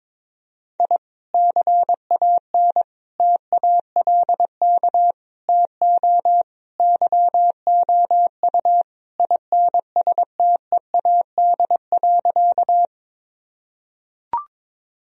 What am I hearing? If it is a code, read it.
Morse code, heard as a keyed tone, about 22 wpm, 700 Hz. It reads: I CAN TALK TO YOU INSTEAD.
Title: Sentences from Top 1000 Words - Review Part 4 - 22wpm